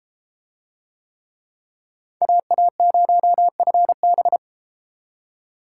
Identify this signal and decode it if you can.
Morse code, heard as a keyed tone, about 33 wpm, 700 Hz. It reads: AA0FB